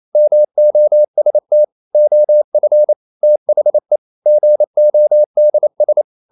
Morse code, heard as a keyed tone, about 28 wpm, 600 Hz. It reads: MOST OF THE GODS
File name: CW / Practice / EnglishQuotations0128